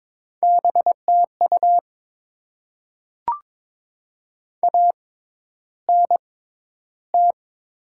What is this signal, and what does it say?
Morse code, heard as a keyed tone, about 22 wpm, spaced as for 11 wpm, 700 Hz. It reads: BTU ANT
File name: Top 20 QSO Elements - 6x Character Spacing - 22wpm